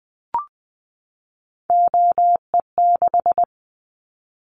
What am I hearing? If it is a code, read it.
Morse code, heard as a keyed tone, about 19 wpm, 700 Hz. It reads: OE6
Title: Callsign Prefixes for Top 50 DXCC - 20wpm